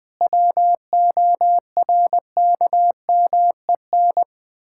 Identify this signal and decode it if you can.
Morse code, heard as a keyed tone, about 21 wpm, 700 Hz. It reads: WORKMEN
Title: Pangram Challenge v2 - 20wpm